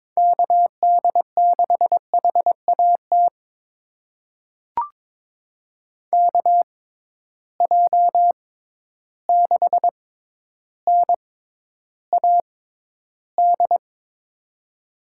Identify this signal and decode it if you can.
Morse code, heard as a keyed tone, about 22 wpm, spaced as for 11 wpm, 700 Hz. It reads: KD6HAT KJ6NAD
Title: USA Call Signs - 6 Characters - 6x Character Spacing - 22wpm